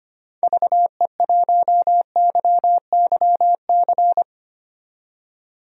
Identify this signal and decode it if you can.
Morse code, heard as a keyed tone, about 25 wpm, 690 Hz. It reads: VE1YYC
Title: DXCC-100 Call Signs - 6 Characters - 25wpm